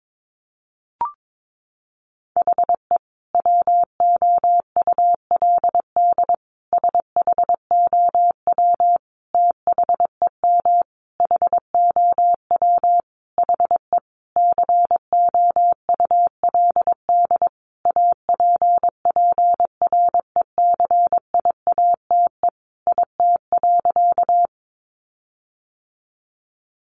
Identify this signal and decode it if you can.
Morse code, heard as a keyed tone, about 22 wpm, 700 Hz. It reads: HE WOULD SHOW THEM HOW HE COULD APPRECIATE IT.
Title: Sentences from Top 1300 Words 22wpm